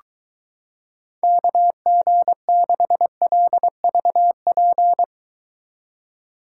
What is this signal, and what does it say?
Morse code, heard as a keyed tone, about 23 wpm, 700 Hz. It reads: KG6LVP